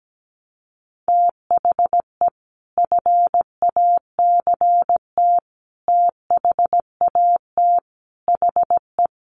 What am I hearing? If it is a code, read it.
Morse code, heard as a keyed tone, about 17 wpm, 700 Hz. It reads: THE FACT THAT HE
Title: Sets of 5 Words 17wpm